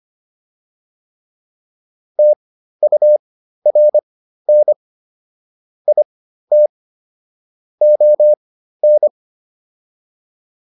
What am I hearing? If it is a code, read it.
Morse code, heard as a keyed tone, about 25 wpm, spaced as for 13 wpm, 600 Hz. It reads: TURN IT ON